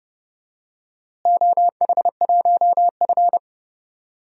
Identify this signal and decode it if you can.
Morse code, heard as a keyed tone, about 31 wpm, 700 Hz. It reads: OH1F
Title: DXCC-100 Call Signs - 4 Characters - Repeated 3x - 30wpm